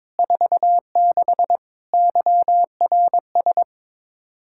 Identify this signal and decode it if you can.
Morse code, heard as a keyed tone, about 21 wpm, 700 Hz. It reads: 46 YRS
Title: Practice QSO - Exchange 3 - More Info - 22wpm